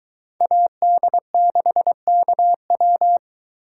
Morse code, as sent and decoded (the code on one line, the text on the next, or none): .- -.. -.... -.- .--
AD6KW